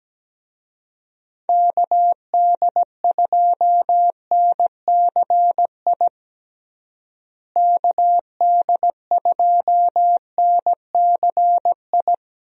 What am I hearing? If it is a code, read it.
Morse code, heard as a keyed tone, about 17 wpm, 700 Hz. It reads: KD2NCI KD2NCI
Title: USA Call Signs - 6 Characters - Repeated 2x - 17wpm